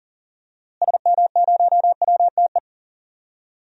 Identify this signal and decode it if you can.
Morse code, heard as a keyed tone, about 40 wpm, 700 Hz. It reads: SM0WTE
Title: DXCC-100 Call Signs - 6 Characters - Repeated 2x - 40wpm